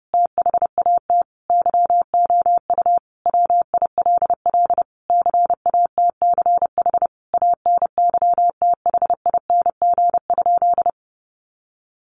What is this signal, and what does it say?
Morse code, heard as a keyed tone, about 28 wpm, 700 Hz. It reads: THAT YOU WILL CATCH ANYTHING?
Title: Sentences from Top 1500 Words 30wpm